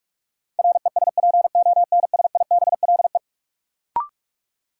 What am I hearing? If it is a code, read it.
Morse code, heard as a keyed tone, about 46 wpm, 700 Hz. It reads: RESPONSIBLE
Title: Sentences from Top 1600 Words - New Word Drill Alt - 45wpm